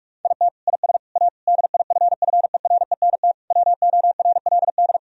Morse code, heard as a keyed tone, about 45 wpm, 700 Hz. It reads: IT IS A DIFFERENT WORLD
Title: Sentences from Top 1000 Words - Review Part 3 - 45wpm